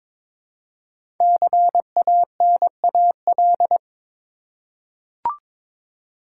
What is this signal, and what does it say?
Morse code, heard as a keyed tone, about 22 wpm, 700 Hz. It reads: CANAL
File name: 5 Letter Words - First 3 Letters Given - 22wpm